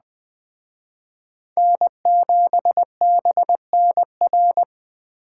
Morse code, heard as a keyed tone, about 20 wpm, 700 Hz. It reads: N7BNR